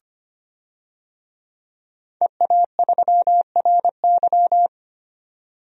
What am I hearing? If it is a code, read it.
Morse code, heard as a keyed tone, about 25 wpm, 700 Hz. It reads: EA3RY
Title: DXCC-100 Call Signs - 5 Characters - 25wpm